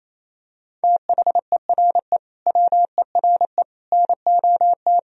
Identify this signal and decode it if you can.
Morse code, heard as a keyed tone, about 28 wpm, 700 Hz. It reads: THERE WERE NOT
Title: Sentences from Top 1000 Words - Review Part 1 - 28wpm